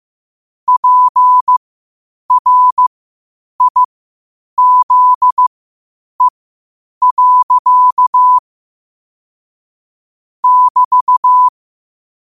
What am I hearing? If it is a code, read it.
Morse code, heard as a keyed tone, about 15 wpm, spaced as for 8 wpm, 1000 Hz. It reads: PRIZE. <BT>
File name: qotdfiles / 8wpm